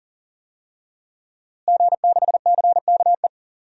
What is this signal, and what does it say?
Morse code, heard as a keyed tone, about 38 wpm, 700 Hz. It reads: G6CKE
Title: DXCC-100 Call Signs - 5 Characters - Repeated 2x - 40wpm